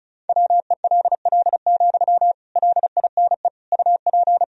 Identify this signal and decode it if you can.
Morse code, heard as a keyed tone, about 36 wpm, 700 Hz. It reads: WELL, LINE UP
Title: Sentences from Top 1000 Words - Review Part 1 - 35wpm